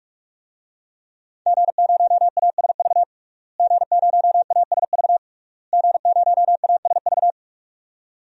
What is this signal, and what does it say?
Morse code, heard as a keyed tone, about 41 wpm, 700 Hz. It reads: G0ASV G0ASV G0ASV
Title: DXCC-100 Call Signs - 5 Characters - Repeated 3x - 45wpm